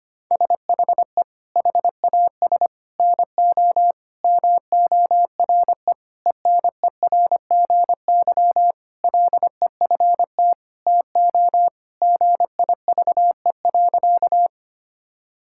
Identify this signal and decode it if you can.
Morse code, heard as a keyed tone, about 25 wpm, 700 Hz. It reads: SHE HAS NO MORE ENERGY LEFT TO GIVE.
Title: Sentences from Top 1000 Words - Review Part 6 - 25wpm